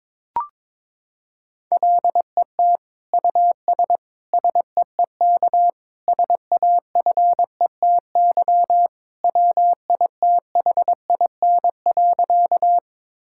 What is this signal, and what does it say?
Morse code, heard as a keyed tone, about 22 wpm, 700 Hz. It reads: LET US SEEK SAFETY WITHIN.
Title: Sentences from Top 1600 Words 22wpm